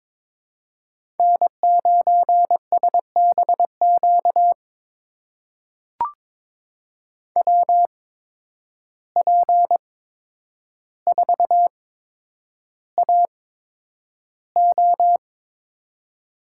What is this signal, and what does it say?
Morse code, heard as a keyed tone, about 22 wpm, spaced as for 9 wpm, 700 Hz. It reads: N9SBQ WP4AO
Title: USA Call Signs - 5 Characters - 8x Character Spacing - 22wpm